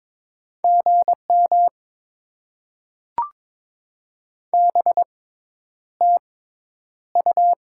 Morse code, heard as a keyed tone, about 22 wpm, spaced as for 10 wpm, 700 Hz. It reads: GM BTU
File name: Top 20 QSO Elements - 6x Character Spacing - 22wpm